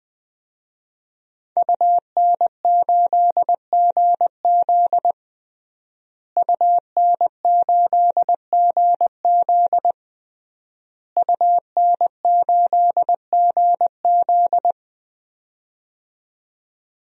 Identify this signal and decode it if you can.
Morse code, heard as a keyed tone, about 20 wpm, 700 Hz. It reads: UN8GZ UN8GZ UN8GZ
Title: DXCC-100 Call Signs - 5 Characters - Repeated 3x - 20wpm